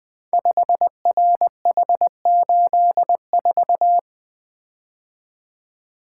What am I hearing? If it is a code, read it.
Morse code, heard as a keyed tone, about 20 wpm, 700 Hz. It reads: 5RH84
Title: Code Groups - 5 Letter:Numbers - 20wpm